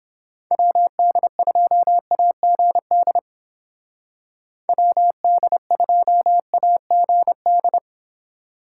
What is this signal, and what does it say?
Morse code, heard as a keyed tone, about 29 wpm, 700 Hz. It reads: WD2AGD WD2AGD